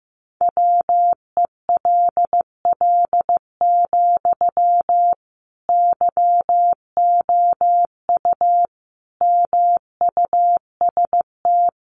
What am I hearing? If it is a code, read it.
Morse code, heard as a keyed tone, about 14 wpm, 700 Hz. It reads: WELL, YOU MUST
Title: Sentences from Top 700 Words 15wpm